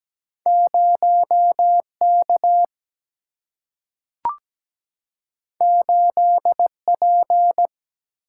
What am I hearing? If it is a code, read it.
Morse code, heard as a keyed tone, about 17 wpm, 700 Hz. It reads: 0 K 8 P
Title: Single-Letter Single-Number 17wpm